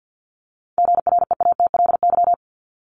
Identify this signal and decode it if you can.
Morse code, heard as a keyed tone, about 50 wpm, 700 Hz. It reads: GREATLY
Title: LY Suffix - Top 100 Words with Base Words 50wpm